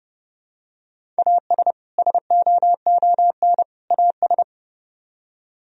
Morse code, heard as a keyed tone, about 29 wpm, 700 Hz. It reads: AS SOON AS